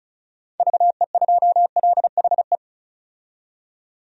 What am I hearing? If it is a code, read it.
Morse code, heard as a keyed tone, about 36 wpm, 700 Hz. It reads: VE2LHE
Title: DXCC-100 Call Signs - 6 Characters - 35wpm